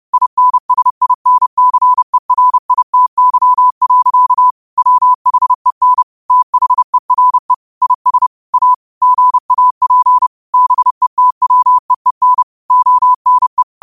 Morse code, heard as a keyed tone, about 30 wpm, 1000 Hz. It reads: INSINCERITY. WHEN THERE IS A GAP BETWEEN ONE
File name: qotdfiles / 30wpm